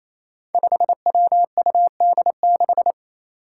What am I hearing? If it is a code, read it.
Morse code, heard as a keyed tone, about 27 wpm, 700 Hz. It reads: 5WUD6
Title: Five Challenging Letter-Number Combinations 28wpm